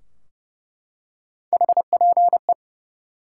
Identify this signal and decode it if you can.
Morse code, heard as a keyed tone, about 30 wpm, 700 Hz. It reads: HPE
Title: Top 90 QSO Elements 30wpm